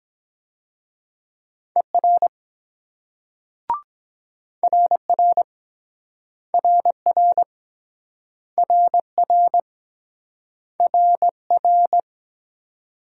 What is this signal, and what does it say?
Morse code, heard as a keyed tone, about 23 wpm, 700 Hz. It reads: ER RR RR RR RR